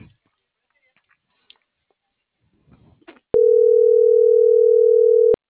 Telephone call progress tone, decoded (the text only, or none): ringback tone